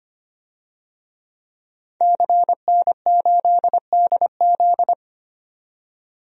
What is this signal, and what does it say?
Morse code, heard as a keyed tone, about 25 wpm, 700 Hz. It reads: CN8DZ
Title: DXCC-100 Call Signs - 5 Characters - 25wpm